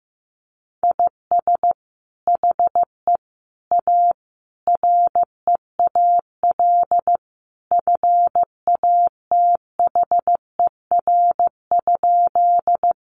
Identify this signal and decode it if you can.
Morse code, heard as a keyed tone, about 15 wpm, 700 Hz. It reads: IS HE A REAL FATHER?
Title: Sentences from Top 300 Words 15wpm